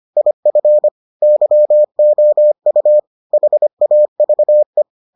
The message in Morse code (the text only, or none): IF YOU HAVE